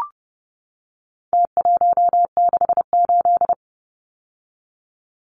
Morse code, heard as a keyed tone, about 30 wpm, 700 Hz. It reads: T168
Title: Numbers - Contest Style - 4-Digit 30wpm